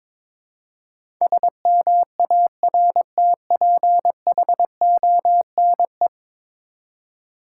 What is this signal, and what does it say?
Morse code, heard as a keyed tone, about 22 wpm, 700 Hz. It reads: SMARTPHONE
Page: morse-code-ninja-practice-22wpm